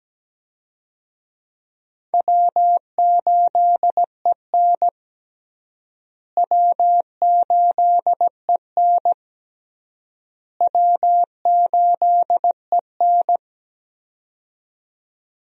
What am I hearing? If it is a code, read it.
Morse code, heard as a keyed tone, about 17 wpm, 700 Hz. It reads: W8EN W8EN W8EN
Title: USA Call Signs - 4 Characters - Repeated 3x - 17wpm